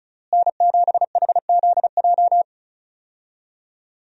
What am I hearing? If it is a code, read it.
Morse code, heard as a keyed tone, about 36 wpm, 700 Hz. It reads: N7HZJ
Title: USA Call Signs - 5 Characters - Repeated 3x - 35wpm